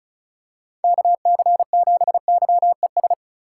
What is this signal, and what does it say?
Morse code, heard as a keyed tone, about 34 wpm, 700 Hz. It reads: KC7YES
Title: USA Call Signs - 6 Characters - 3x Character Spacing - 35wpm